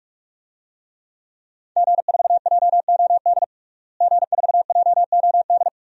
Morse code, heard as a keyed tone, about 46 wpm, 700 Hz. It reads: G4JOD G4JOD
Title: DXCC-100 Call Signs - 5 Characters - Repeated 3x - 45wpm